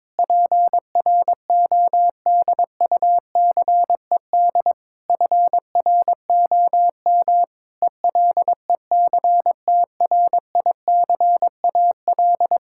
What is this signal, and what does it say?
Morse code, heard as a keyed tone, about 22 wpm, 700 Hz. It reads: PRODUCED FROM ELECTRICAL
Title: Sentences from Top 1000 Words - Review Part 6 - 22wpm